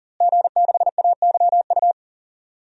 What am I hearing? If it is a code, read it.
Morse code, heard as a keyed tone, about 40 wpm, 700 Hz. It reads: G6AYU
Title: DXCC-100 Call Signs - 5 Characters - Repeated 3x - 40wpm